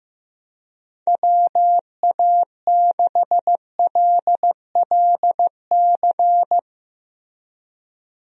Morse code, heard as a keyed tone, about 15 wpm, 700 Hz. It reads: WA6LLC
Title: USA Call Signs - 6 Characters - Repeated 3x - 15wpm